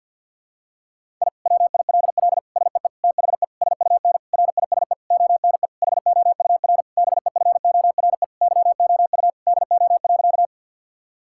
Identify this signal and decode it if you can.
Morse code, heard as a keyed tone, about 52 wpm, 700 Hz. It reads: I WILL SEE THE SUN RISE ONE HOUR BEFORE YOU DO.
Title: Sentences from Top 1000 Words - Review Part 6 - 50wpm